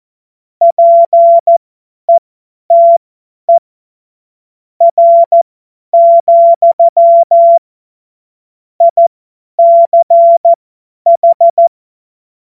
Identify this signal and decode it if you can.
Morse code, heard as a keyed tone, about 14 wpm, spaced as for 9 wpm, 680 Hz. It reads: PETER, ICH
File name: Morsekurs / Witze / Witz 05. Oktober 2021, 50 F70